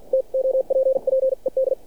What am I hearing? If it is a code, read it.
Morse code, heard as a keyed tone, about 49 wpm, spaced as for 41 wpm, 530 Hz. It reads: TOPPED
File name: archived / CRSnet